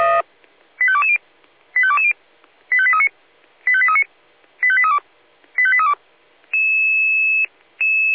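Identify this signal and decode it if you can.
Five-tone selective call (ZVEI), heard as quick runs of five tones, tones of about 70 ms. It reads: 85220 85220 85839 85839 85832 85832
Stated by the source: ZVEI five-tone call received by SDR.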